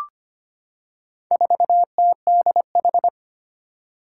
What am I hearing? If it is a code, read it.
Morse code, heard as a keyed tone, about 25 wpm, 700 Hz. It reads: 4TDH
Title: Four Challenging Letter-Number Combinations 25wpm